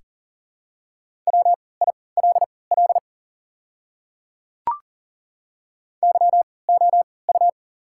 Morse code, heard as a keyed tone, about 40 wpm, spaced as for 23 wpm, 700 Hz. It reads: WILL YOU
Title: 3x Character Spacing - Top 200 Words - 40wpm